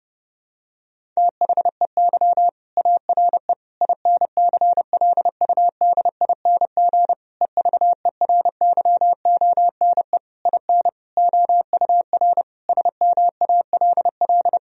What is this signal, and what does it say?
Morse code, heard as a keyed tone, about 32 wpm, 700 Hz. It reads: THEY ARE INCLUDING EVERYONE IN OUR SMALL
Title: Sentences from Top 1000 Words - Review Part 4 - 30wpm